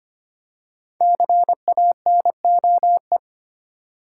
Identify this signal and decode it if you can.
Morse code, heard as a keyed tone, about 30 wpm, 700 Hz. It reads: CANOE
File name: Easy Words - Speed-Racing with Hints - 25wpm